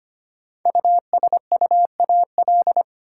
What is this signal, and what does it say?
Morse code, heard as a keyed tone, about 25 wpm, 700 Hz. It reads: USUAL
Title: Sentences from Top 400 Words - New Word Drill - 25wpm